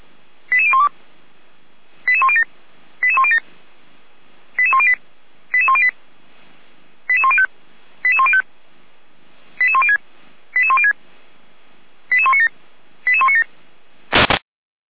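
Five-tone selective call (ZVEI), heard as quick runs of five tones, tones of about 80 ms. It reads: 88012 88107 88107 88108 88108 88105 88105 88106 88106 88197 88197
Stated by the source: ZVEI five-tone selective call received by SDR, logged on a Feuerwehr Orlamünde fire brigade server.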